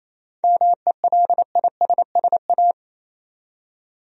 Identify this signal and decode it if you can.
Morse code, heard as a keyed tone, about 28 wpm, 700 Hz. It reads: MELISSA
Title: USA Common Names - 28wpm